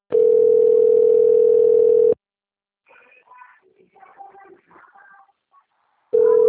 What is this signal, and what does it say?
Ringback tone (North America), 440 + 480 Hz.
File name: rg-1600-unknown- - - .30398